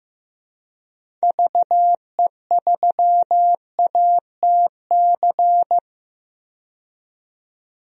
Morse code, heard as a keyed tone, about 15 wpm, 700 Hz. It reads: VE3ATC
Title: DXCC-100 Call Signs - 6 Characters - 15wpm